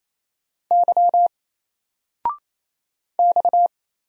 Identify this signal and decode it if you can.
Morse code, heard as a keyed tone, about 28 wpm, 700 Hz. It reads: Y X
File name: Single-Letter-Number-28wpm